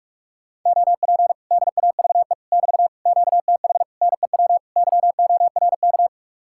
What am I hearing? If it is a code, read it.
Morse code, heard as a keyed tone, about 50 wpm, 700 Hz. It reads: OP DAVE <BT> QTH NEW YORK